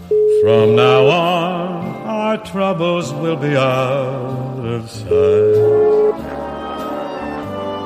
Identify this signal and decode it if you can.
Ringback tone (Europe, 425 Hz), with speech.